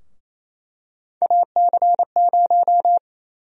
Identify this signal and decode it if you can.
Morse code, heard as a keyed tone, about 28 wpm, 700 Hz. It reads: AC0
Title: USA Call Sign Prefixes - 28wpm